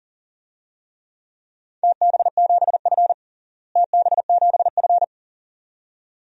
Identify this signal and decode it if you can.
Morse code, heard as a keyed tone, about 40 wpm, 700 Hz. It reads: TB7F TB7F
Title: DXCC-100 Call Signs - 4 Characters - Repeated 2x - 40wpm